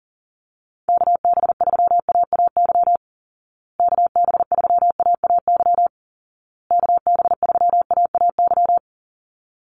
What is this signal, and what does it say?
Morse code, heard as a keyed tone, about 40 wpm, 700 Hz. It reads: KB3AAY KB3AAY KB3AAY
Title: Call Signs Repeated 2x - 40wpm